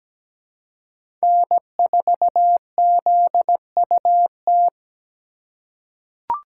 Morse code, heard as a keyed tone, about 17 wpm, 700 Hz. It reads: N4ZUT
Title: USA Call Signs - 5 Characters - 3x Character Spacing - 17wpm